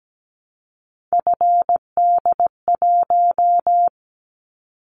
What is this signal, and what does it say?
Morse code, heard as a keyed tone, about 17 wpm, 700 Hz. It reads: FD1
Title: Callsign Prefixes for Top 50 DXCC - 17wpm